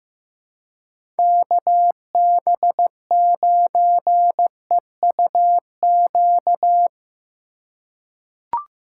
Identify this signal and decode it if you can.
Morse code, heard as a keyed tone, about 15 wpm, 700 Hz. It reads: KB9EUQ
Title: USA Call Signs - 6 Characters - 2x Character Spacing - 15wpm